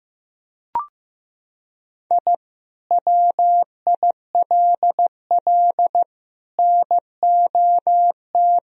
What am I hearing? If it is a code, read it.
Morse code, heard as a keyed tone, about 15 wpm, 700 Hz. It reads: I WILL NOT
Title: Sentences from Top 1000 Words - Review Part 2 - 15wpm